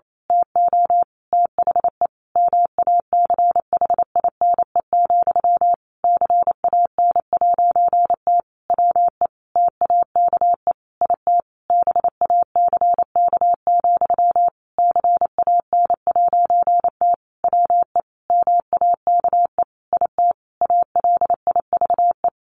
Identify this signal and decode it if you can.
Morse code, heard as a keyed tone, about 28 wpm, 700 Hz. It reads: TO THE MACHINE, CAN'T WE TAKE IT BACK, CAN'T WE MAKE IT ALIVE
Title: A Sound Of Thunder 28wpm